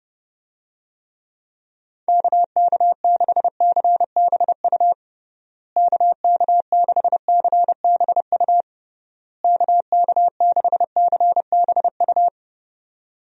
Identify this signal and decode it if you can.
Morse code, heard as a keyed tone, about 30 wpm, 700 Hz. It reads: KK6CBU KK6CBU KK6CBU
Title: USA Call Signs - 6 Characters - Repeated 3x - 30wpm